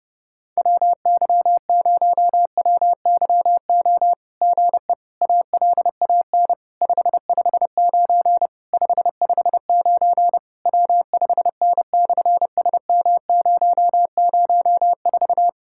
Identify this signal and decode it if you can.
Morse code, heard as a keyed tone, about 30 wpm, 700 Hz. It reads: WY0WYO GE ALAN 559 559 W5N/SM004
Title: SOTA QSO Practice 30wpm